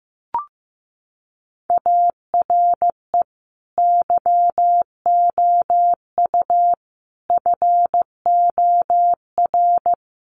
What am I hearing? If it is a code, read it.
Morse code, heard as a keyed tone, about 15 wpm, 700 Hz. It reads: ARE YOU FOR
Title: Sentences from Top 1100 Words 15wpm